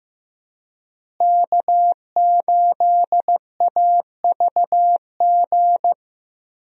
Morse code, heard as a keyed tone, about 15 wpm, 700 Hz. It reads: K8AVG
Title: USA Call Signs - 5 Characters - Repeated 3x - 15wpm